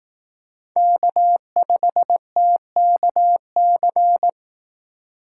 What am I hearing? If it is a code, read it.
Morse code, heard as a keyed tone, about 18 wpm, 700 Hz. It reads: K5TKC
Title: Speed Racing - USA 1x3 Call Signs 15wpm